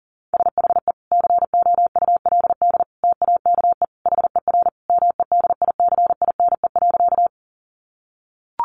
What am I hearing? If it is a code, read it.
Morse code, heard as a keyed tone, about 40 wpm, 700 Hz. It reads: SHE COULD TAKE HER MEDICINE.
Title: Sentences from Top 900 Words 40wpm